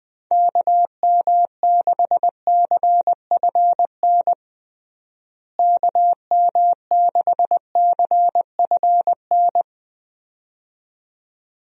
Morse code, heard as a keyed tone, about 20 wpm, 700 Hz. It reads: KM6CFN KM6CFN
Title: USA Call Signs - 6 Characters - Repeated 2x - 20wpm